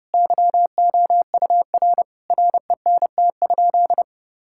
Morse code, heard as a keyed tone, about 30 wpm, 700 Hz. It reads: YOUR RENT?
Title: Sentences from Top 1000 Words - Review Part 5 - 30wpm